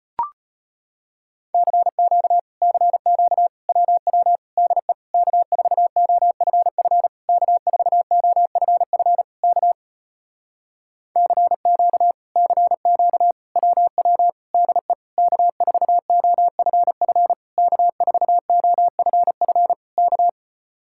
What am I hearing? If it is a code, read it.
Morse code, heard as a keyed tone, about 33 wpm, 700 Hz. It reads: CQ CQ WW DE K4OFF K4OFF K CQ CQ WW DE K4OFF K4OFF K